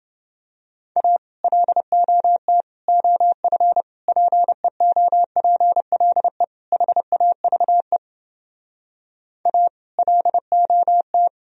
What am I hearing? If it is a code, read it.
Morse code, heard as a keyed tone, about 30 wpm, 700 Hz. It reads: A LOT OF PEOPLE HAVE A LOT